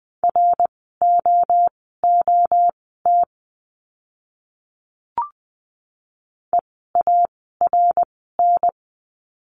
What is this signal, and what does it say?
Morse code, heard as a keyed tone, about 20 wpm, spaced as for 14 wpm, 700 Hz. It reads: ROOT EARN
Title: 054 Lesson 10f - Letter r - Words 20wpm